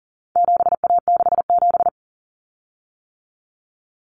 Morse code, heard as a keyed tone, about 40 wpm, 700 Hz. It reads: 7A67
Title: Common Cut Numbers - 4-Digit 40wpm